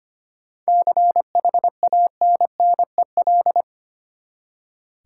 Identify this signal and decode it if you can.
Morse code, heard as a keyed tone, about 25 wpm, 700 Hz. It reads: CHANNEL
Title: Seven Letter Words - 25wpm